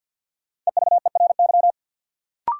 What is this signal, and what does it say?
Morse code, heard as a keyed tone, about 53 wpm, 700 Hz. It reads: EVERY
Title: Sets of 5 Words - New Word Drill - 50wpm